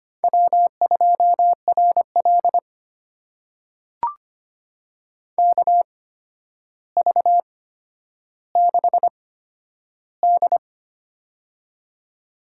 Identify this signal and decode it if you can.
Morse code, heard as a keyed tone, about 25 wpm, spaced as for 11 wpm, 700 Hz. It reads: W2RL KV6D
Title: USA Call Signs - 4 Characters - 8x Character Spacing - 25wpm